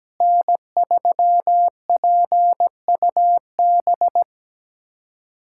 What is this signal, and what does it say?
Morse code, heard as a keyed tone, about 17 wpm, 700 Hz. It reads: N3PUB
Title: USA Callsign DE Callsign - 5 Characters - 17wpm